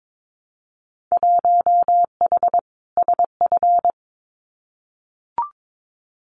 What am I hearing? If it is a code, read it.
Morse code, heard as a keyed tone, about 22 wpm, 700 Hz. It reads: 1H SF
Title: Winter Field Day Exchanges 22wpm